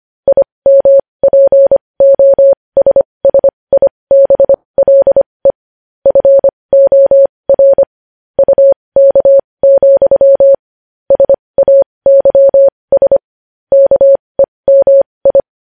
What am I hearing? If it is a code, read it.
Morse code, heard as a keyed tone, about 25 wpm, spaced as for 20 wpm, 550 Hz. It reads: IMPOSSIBLE FOR UK, SAYS KEMI
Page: news-headlines-in-morse-code-at-20-wpm